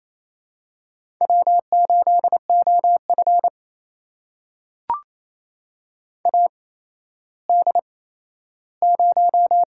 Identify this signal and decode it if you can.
Morse code, heard as a keyed tone, about 27 wpm, spaced as for 13 wpm, 700 Hz. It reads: W8OF AD0
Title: USA Call Signs - 8x Character Spacing - 28wpm